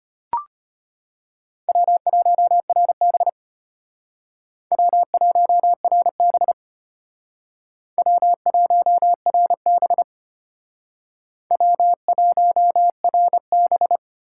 Morse code, heard as a keyed tone, about 32 wpm, 700 Hz. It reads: W1RB W1RB W1RB W1RB